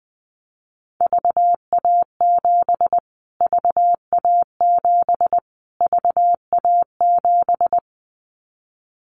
Morse code, heard as a keyed tone, about 20 wpm, 700 Hz. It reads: VA7 VA7 VA7
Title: Callsign Prefixes for Top 25 DXCC - Repeated 3x - 20wpm